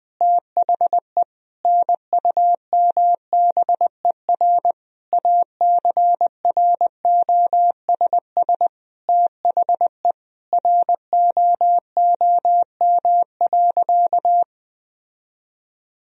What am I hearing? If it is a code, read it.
Morse code, heard as a keyed tone, about 20 wpm, 700 Hz. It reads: THE NUMBER ACROSS THE ROOM.
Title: Sentences from Top 1000 Words - Review Part 1 - 20wpm